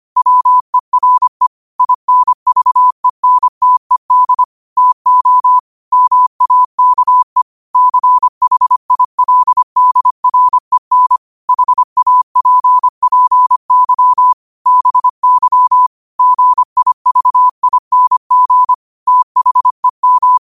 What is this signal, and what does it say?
Morse code, heard as a keyed tone, about 25 wpm, 1000 Hz. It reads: WERE INVENTED TO MAKE CHILDREN HAPPY BY GIVING THEM
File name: qotdfiles / 25wpm